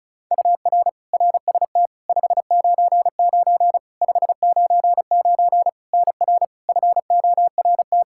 Morse code, heard as a keyed tone, about 36 wpm, 700 Hz. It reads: UR RST 599 599 NR FORT
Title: Practice QSO - Exchange 1 - Beginning - 35wpm